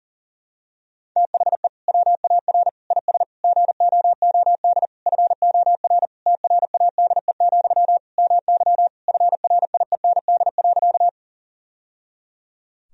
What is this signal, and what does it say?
Morse code, heard as a keyed tone, about 40 wpm, 700 Hz. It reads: THE WAR IS GOOD FOR TRADE, MY FRIEND.
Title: Sentences from Top 1000 Words - Review Part 4 - 40wpm